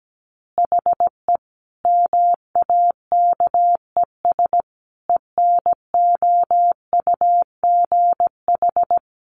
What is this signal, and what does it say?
Morse code, heard as a keyed tone, about 17 wpm, 700 Hz. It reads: HE MAKES ENOUGH
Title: Additional Easy Phrases 17wpm